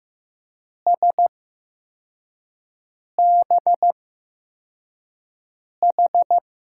Morse code, heard as a keyed tone, about 15 wpm, spaced as for 4 wpm, 700 Hz. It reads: SBH
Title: USA Call Signs - 8x Character Spacing - 15wpm